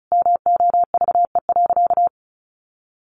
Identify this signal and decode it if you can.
Morse code, heard as a keyed tone, about 34 wpm, 700 Hz. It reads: MOVE.
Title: A Sound Of Thunder 35wpm